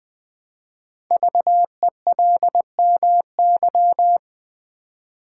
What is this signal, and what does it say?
Morse code, heard as a keyed tone, about 20 wpm, 700 Hz. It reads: VELMY
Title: Code Groups - 5 Letters - 20wpm